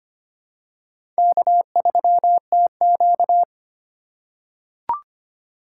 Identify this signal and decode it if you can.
Morse code, heard as a keyed tone, about 25 wpm, 700 Hz. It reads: K3TQ
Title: USA Call Signs - 4 Characters - 2x Character Spacing - 25wpm